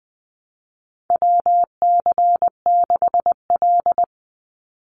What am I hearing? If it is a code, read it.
Morse code, heard as a keyed tone, about 20 wpm, 700 Hz. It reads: WC6L
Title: USA Call Signs 20wpm